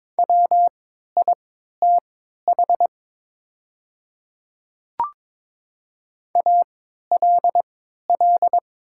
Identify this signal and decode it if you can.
Morse code, heard as a keyed tone, about 23 wpm, spaced as for 13 wpm, 700 Hz. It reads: WITH ALL
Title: 3x Character Spacing - Top 100 Words - 22wpm